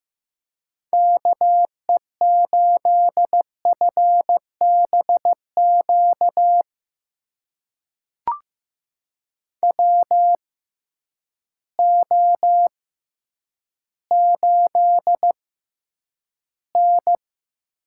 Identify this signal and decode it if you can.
Morse code, heard as a keyed tone, about 15 wpm, spaced as for 9 wpm, 700 Hz. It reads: KE8FBQ WO8N
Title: USA Call Signs - 6x Character Spacing - 15wpm